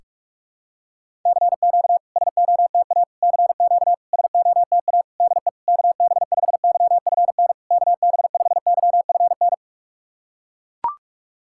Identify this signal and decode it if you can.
Morse code, heard as a keyed tone, about 41 wpm, 690 Hz. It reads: CQ SOTA CQ SOTA DE KB5YFN KB5YFN
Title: SOTA QSO Practice 45wpm